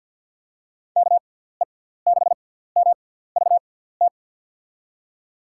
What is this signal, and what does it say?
Morse code, heard as a keyed tone, about 51 wpm, spaced as for 17 wpm, 700 Hz. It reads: KE6MVT
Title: USA Call Signs - 6 Characters - 6x Character Spacing - 50wpm